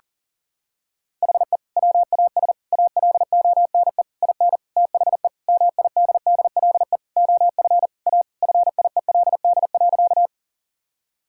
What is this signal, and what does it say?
Morse code, heard as a keyed tone, about 40 wpm, 700 Hz. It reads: HE WAS ALONE IN THE MIDDLE OF A FIELD.
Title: Sentences from Top 1000 Words - Review Part 6 - 40wpm